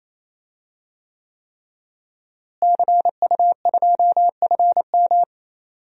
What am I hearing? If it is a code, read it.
Morse code, heard as a keyed tone, about 28 wpm, 700 Hz. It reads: CU2FM
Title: DXCC-100 Call Signs - 5 Characters - Repeated 3x - 28wpm